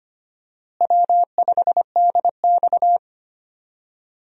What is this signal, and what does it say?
Morse code, heard as a keyed tone, about 25 wpm, 700 Hz. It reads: W5DX
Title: USA Call Signs - 4 Characters - Repeated 2x - 25wpm